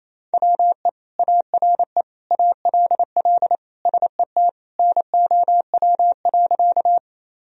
Morse code, heard as a keyed tone, about 28 wpm, 700 Hz. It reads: WE ARE ALL SET NOW.
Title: Sentences from Top 1000 Words - Review Part 1 - 28wpm